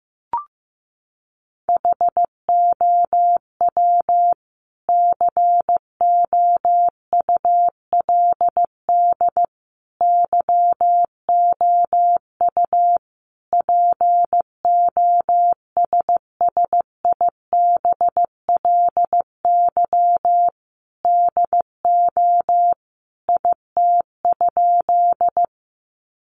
Morse code, heard as a keyed tone, about 15 wpm, 700 Hz. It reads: HOW COULD YOU POSSIBLY DO IT?
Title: Sentences from Top 1300 Words 15wpm